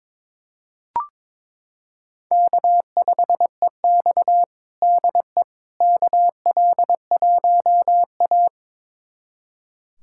Morse code, heard as a keyed tone, about 22 wpm, 700 Hz. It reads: K5EX DE KL1A
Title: USA Callsign DE Callsign - 4 Characters - 22wpm